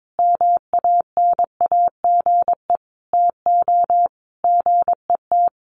Morse code, heard as a keyed tone, about 22 wpm, 700 Hz. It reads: MANAGE TO GET
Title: Sentences from Top 1400 Words 22wpm